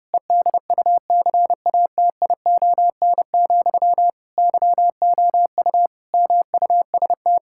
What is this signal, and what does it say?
Morse code, heard as a keyed tone, about 30 wpm, 700 Hz. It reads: EDUCATION, YOU MUST
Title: Sentences from Top 1000 Words - Review Part 6 - 30wpm